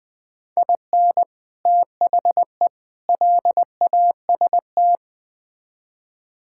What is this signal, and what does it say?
Morse code, heard as a keyed tone, about 20 wpm, 700 Hz. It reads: IN THE LAST
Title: Build A Phrase - 5-Word Phrases - 20wpm